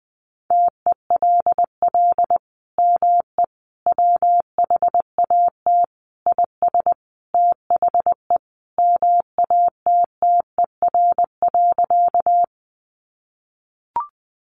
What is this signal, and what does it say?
Morse code, heard as a keyed tone, about 20 wpm, 700 Hz. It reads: TELL ME WHAT IS THE MATTER.
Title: Sentences from Top 700 Words 20wpm